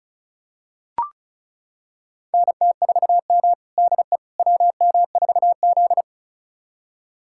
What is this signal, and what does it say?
Morse code, heard as a keyed tone, about 34 wpm, 700 Hz. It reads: NT4M DE WM4Z
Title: USA Callsign DE Callsign - 4 Characters - 35wpm